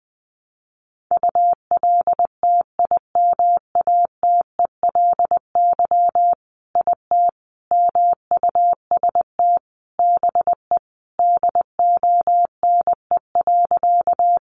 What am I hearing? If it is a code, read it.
Morse code, heard as a keyed tone, about 19 wpm, 700 Hz. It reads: ULTIMATELY IT MUST BE DONE.
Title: Sentences from Top 1200 Words 20wpm